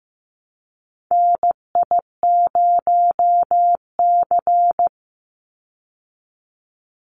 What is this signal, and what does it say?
Morse code, heard as a keyed tone, about 15 wpm, 700 Hz. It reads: NI0C
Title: Call Signs Repeated 2x - 15wpm